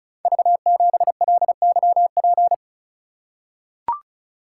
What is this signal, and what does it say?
Morse code, heard as a keyed tone, about 36 wpm, 700 Hz. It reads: V7LYP
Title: Code Groups - 5 Letter:Numbers - 35wpm